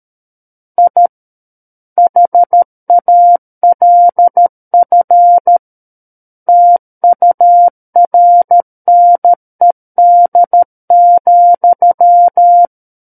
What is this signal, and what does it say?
Morse code, heard as a keyed tone, about 13 wpm, 700 Hz. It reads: I HALF TURNED,
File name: Book One / 04-Chapter 4 - 13 WPM